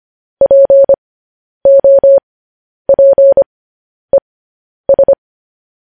Morse code, heard as a keyed tone, about 25 wpm, spaced as for 10 wpm, 550 Hz. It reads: POPES
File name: podcast / News headlines-1x-25x10